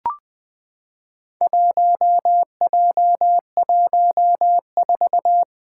Morse code, heard as a keyed tone, about 20 wpm, 700 Hz. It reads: 1J14
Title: Four Challenging Letter-Number Combinations 20wpm